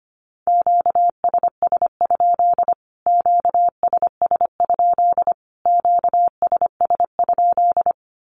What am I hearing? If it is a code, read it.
Morse code, heard as a keyed tone, about 25 wpm, 700 Hz. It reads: QSS? QSS? QSS?
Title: Q-Codes Repeated 3x 25wpm